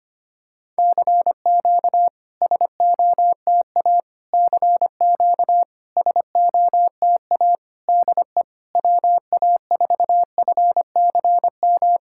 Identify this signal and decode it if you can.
Morse code, heard as a keyed tone, about 25 wpm, 690 Hz. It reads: CQ SOTA CQ SOTA DE WA4FCM
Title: SOTA QSO Practice 25wpm